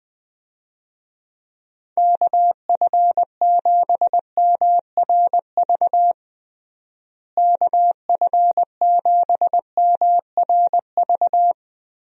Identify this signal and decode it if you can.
Morse code, heard as a keyed tone, about 20 wpm, 700 Hz. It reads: KF7MRV KF7MRV
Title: USA Call Signs - 6 Characters - Repeated 2x - 20wpm